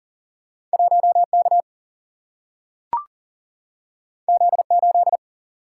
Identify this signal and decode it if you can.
Morse code, heard as a keyed tone, about 40 wpm, 700 Hz. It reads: 1K Z8
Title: Single-Letter Single-Number 40wpm